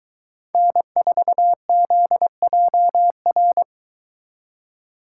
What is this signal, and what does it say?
Morse code, heard as a keyed tone, about 22 wpm, 700 Hz. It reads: N4ZJR